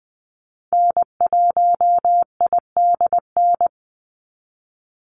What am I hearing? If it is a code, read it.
Morse code, heard as a keyed tone, about 20 wpm, 700 Hz. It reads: N1IDN
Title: NA SOTA Activator Callsigns 20wpm